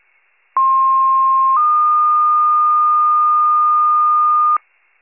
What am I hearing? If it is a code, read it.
Two-tone page, heard as a one-second tone, then a three-second tone, about 1060 then 1220 Hz, with noise. It is heard from 0.6 to 4.6 s.